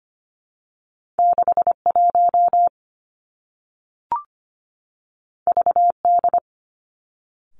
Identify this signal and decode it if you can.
Morse code, heard as a keyed tone, about 25 wpm, 700 Hz. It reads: 61 VD
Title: Two Challenging Letter-Number Combinations 25wpm